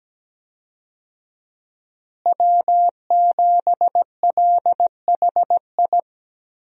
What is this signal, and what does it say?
Morse code, heard as a keyed tone, about 17 wpm, 700 Hz. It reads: W7LHI
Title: USA Call Signs - 5 Characters - Repeated 3x - 17wpm